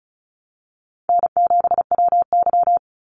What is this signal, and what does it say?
Morse code, heard as a keyed tone, about 34 wpm, 700 Hz. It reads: N7WY
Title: Easy Call Signs Repeated 3x - 35wpm